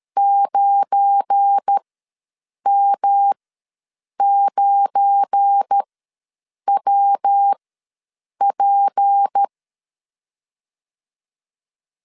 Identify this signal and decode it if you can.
Morse code, heard as a keyed tone, about 13 wpm, spaced as for 7 wpm, 800 Hz. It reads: 9M9WP
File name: LEZIONI / 7 GF9Z / 7 GF9 ws 12wpm 3344